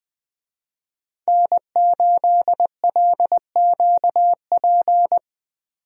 Morse code, heard as a keyed tone, about 21 wpm, 700 Hz. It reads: N8LQP
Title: USA Call Signs - 5 Characters - 20wpm